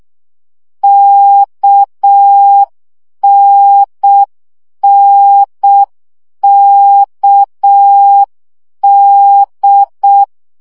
Morse code, heard as a keyed tone, about 6 wpm, 800 Hz. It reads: KNNKD
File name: CW gyakorlas n d k 5 baud